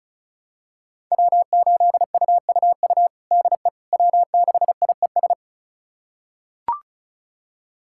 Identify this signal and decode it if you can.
Morse code, heard as a keyed tone, about 34 wpm, 700 Hz. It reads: W8UUU DE W6IES
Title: USA Callsign DE Callsign - 5 Characters - 35wpm